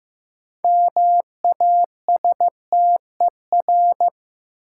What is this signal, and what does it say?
Morse code, heard as a keyed tone, about 15 wpm, 700 Hz. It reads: MASTER
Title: Sentences from Top 1300 Words - New Word Drill - 15wpm